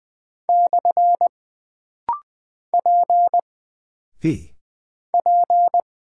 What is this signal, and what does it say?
Morse code, heard as a keyed tone, about 20 wpm, 700 Hz. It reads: / P P
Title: Single Character (Letter:Number:Punctuation:Prosign) - Rapid-Fire - 20wpm